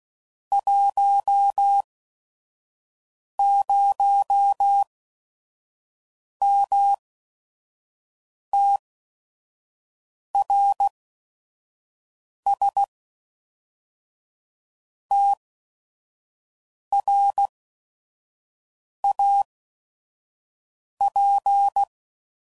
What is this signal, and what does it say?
Morse code, heard as a keyed tone, about 16 wpm, spaced as for 5 wpm, 800 Hz. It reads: 10MTRS TRAP
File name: CW Code Groups @5WPM / QSOs / WD0FBT 2